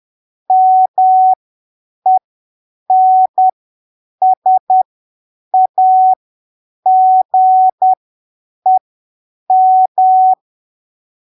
Morse code, heard as a keyed tone, about 10 wpm, spaced as for 7 wpm, 750 Hz. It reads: MENSAGEM